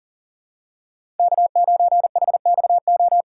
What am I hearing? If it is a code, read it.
Morse code, heard as a keyed tone, about 40 wpm, 700 Hz. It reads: K9HXO
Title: USA Call Signs - 5 Characters - 6x Character Spacing - 40wpm